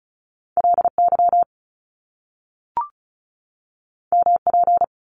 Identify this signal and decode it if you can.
Morse code, heard as a keyed tone, about 36 wpm, 700 Hz. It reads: LY MP
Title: Top Two Letter Combinations in Words 35wpm